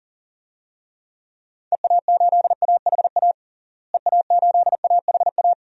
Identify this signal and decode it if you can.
Morse code, heard as a keyed tone, about 37 wpm, 700 Hz. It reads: EA8AHA EA8AHA
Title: DXCC-100 Call Signs - 6 Characters - Repeated 2x - 40wpm